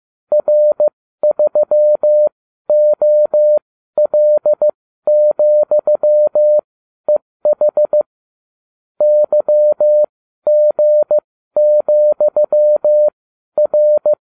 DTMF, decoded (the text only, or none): none